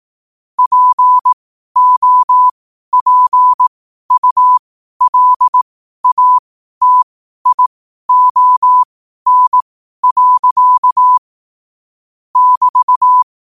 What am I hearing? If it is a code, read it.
Morse code, heard as a keyed tone, about 18 wpm, spaced as for 12 wpm, 1000 Hz. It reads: POPULATION. <BT>
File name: qotdfiles / 12wpm